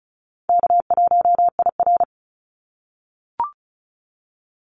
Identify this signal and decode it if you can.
Morse code, heard as a keyed tone, about 34 wpm, 700 Hz. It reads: K1IR
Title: Call Signs - 35wpm